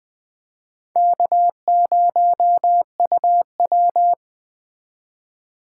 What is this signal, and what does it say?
Morse code, heard as a keyed tone, about 21 wpm, 700 Hz. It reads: K0UW